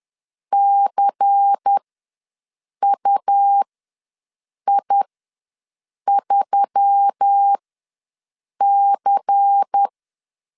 Morse code, heard as a keyed tone, about 10 wpm, spaced as for 6 wpm, 800 Hz. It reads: CUI3C
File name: LEZIONI / 5 LBVC / 5 LBVC ws 10wpm 3733